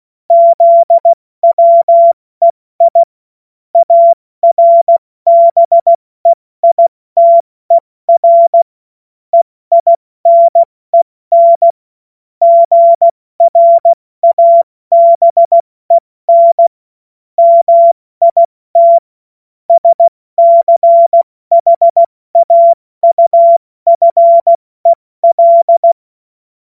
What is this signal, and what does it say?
Morse code, heard as a keyed tone, about 16 wpm, 680 Hz. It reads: ZWEI ARBEITER EINEN GRABEN MIT SCHAUFEL
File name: Morsekurs / Witze / Witz 19 Dezember 2023 70 F80